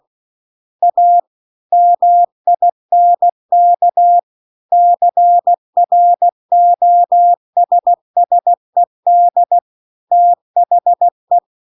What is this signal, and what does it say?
Morse code, heard as a keyed tone, about 16 wpm, 700 Hz. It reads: A MINK CROSSED THE
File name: 16WPM